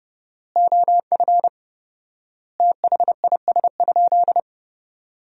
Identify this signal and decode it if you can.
Morse code, heard as a keyed tone, about 30 wpm, 700 Hz. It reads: OF THIS?
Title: 4x Word-Spacing - Sentences from Top 200 Words - 30wpm